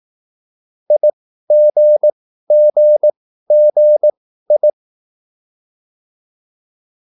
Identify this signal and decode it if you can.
Morse code, heard as a keyed tone, about 18 wpm, spaced as for 13 wpm, 600 Hz. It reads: IGGGI
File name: cwbook